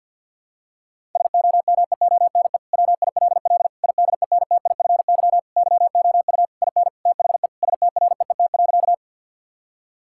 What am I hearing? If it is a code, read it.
Morse code, heard as a keyed tone, about 50 wpm, 700 Hz. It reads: SOMEONE WILL IDENTIFY YOU IN THE STREET.
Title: Sentences from Top 1000 Words - Review Part 6 - 50wpm